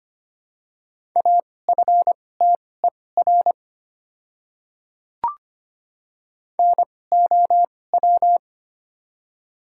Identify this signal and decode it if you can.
Morse code, heard as a keyed tone, about 25 wpm, spaced as for 18 wpm, 700 Hz. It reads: AFTER NOW
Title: 2x Character Spacing - Top 200 Words - 25wpm